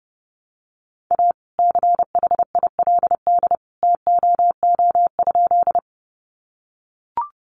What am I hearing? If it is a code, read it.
Morse code, heard as a keyed tone, about 30 wpm, 700 Hz. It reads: A CHILD TOO?
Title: Sentences from Top 800 Words 30wpm